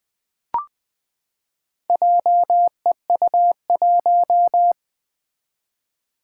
Morse code, heard as a keyed tone, about 21 wpm, 700 Hz. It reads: JEU1
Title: Four Challenging Letter-Number Combinations 20wpm